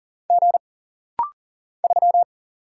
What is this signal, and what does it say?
Morse code, heard as a keyed tone, about 40 wpm, 700 Hz. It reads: G 3
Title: Single Letter-Number Alt - Rapid-Fire 40wpm